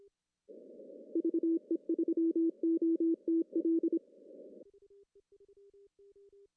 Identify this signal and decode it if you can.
Morse code, heard as a keyed tone, about 26 wpm, 330 Hz. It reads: VE3OTL
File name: CW / cqww-cw-k5tr